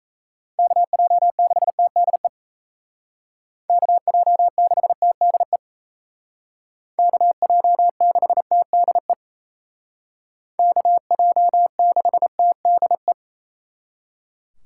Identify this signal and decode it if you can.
Morse code, heard as a keyed tone, about 33 wpm, 700 Hz. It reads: KJ6TDE KJ6TDE KJ6TDE KJ6TDE
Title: Speed Racing - USA 2x3 Call Signs 28wpm